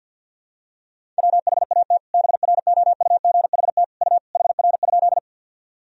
Morse code, heard as a keyed tone, about 52 wpm, 700 Hz. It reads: WHAT BROUGHT U HR?
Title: Sentences from Top 500 Words and Abbreviations 50wpm